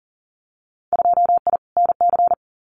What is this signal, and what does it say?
Morse code, heard as a keyed tone, about 42 wpm, 700 Hz. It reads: 2I NC
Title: Winter Field Day Exchanges 40wpm